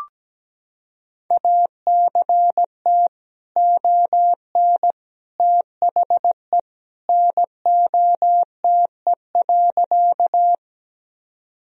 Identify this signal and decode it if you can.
Morse code, heard as a keyed tone, about 17 wpm, 700 Hz. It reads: ACT ON THE NOTE.
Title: Sentences from Top 1000 Words - Review Part 2 - 17wpm